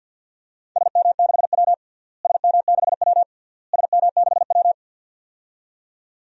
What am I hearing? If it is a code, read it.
Morse code, heard as a keyed tone, about 52 wpm, 700 Hz. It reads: SM6W SM6W SM6W
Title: DXCC-100 Call Signs - 4 Characters - Repeated 3x - 50wpm